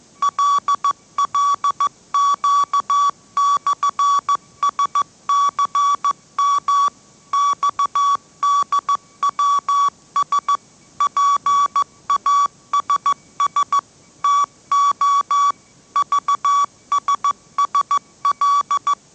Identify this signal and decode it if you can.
Morse code, heard as a keyed tone, about 14 wpm, 1140 Hz. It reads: LLQ/SCM XDWS PASS TO VSSL
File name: mrx / mrx065 16wpm